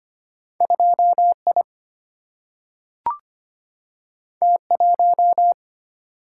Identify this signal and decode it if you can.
Morse code, heard as a keyed tone, about 25 wpm, 700 Hz. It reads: 2I T1
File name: Two Letter-Number Combinations - 25wpm